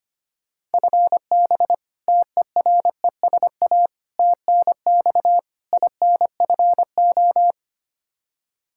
Morse code, heard as a keyed tone, about 25 wpm, 700 Hz. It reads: FB TERESA TNX INFO
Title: Practice QSO - Exchange 3 - More Info - 25wpm